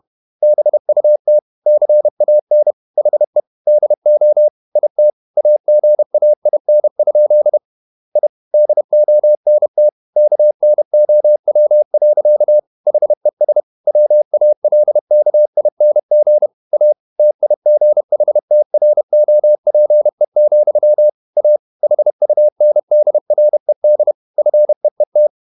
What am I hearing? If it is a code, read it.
Morse code, heard as a keyed tone, about 31 wpm, 600 Hz. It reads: BUT CAN HE DO IT AGAIN? I DONT KNOW. HES WALKING A TIGHTROPE, A HUNDRED FEET